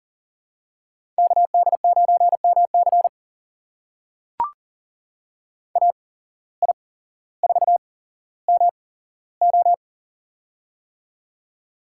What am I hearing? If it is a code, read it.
Morse code, heard as a keyed tone, about 37 wpm, spaced as for 17 wpm, 700 Hz. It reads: KD9MC AI4MO
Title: USA Call Signs - 5 Characters - 8x Character Spacing - 40wpm